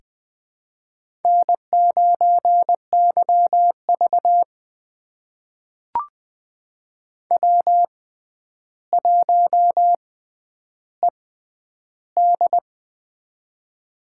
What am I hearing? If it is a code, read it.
Morse code, heard as a keyed tone, about 20 wpm, spaced as for 10 wpm, 700 Hz. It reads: N9YV W1ED
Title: USA Call Signs - 4 Characters - 6x Character Spacing - 20wpm